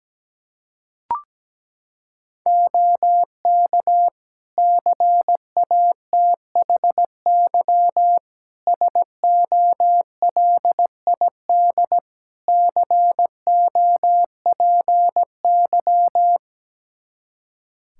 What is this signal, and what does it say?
Morse code, heard as a keyed tone, about 17 wpm, 700 Hz. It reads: OK CATHY SOLID COPY
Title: K1USN SST QSO Practice 17wpm